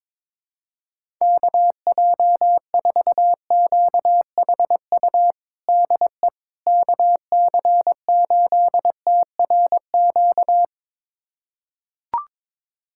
Morse code, heard as a keyed tone, about 22 wpm, 700 Hz. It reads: KJ4QHU DE KC8TRQ
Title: USA Callsign DE Callsign 22wpm